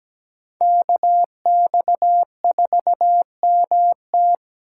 Morse code, heard as a keyed tone, about 20 wpm, 700 Hz. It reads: KX4MT
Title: Speed Racing - USA 2x2 Call Signs 17wpm